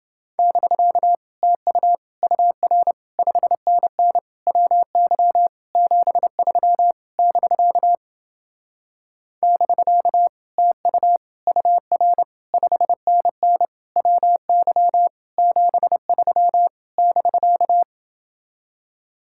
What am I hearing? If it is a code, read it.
Morse code, heard as a keyed tone, about 29 wpm, 700 Hz. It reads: <BK> TU UR 5NN WY 73 <BK> <BK> TU UR 5NN WY 73 <BK>